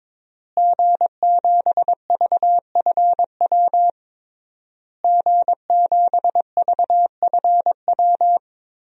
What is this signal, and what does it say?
Morse code, heard as a keyed tone, about 22 wpm, 700 Hz. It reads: G7VFW G7VFW
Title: DXCC-100 Call Signs - 5 Characters - Repeated 3x - 22wpm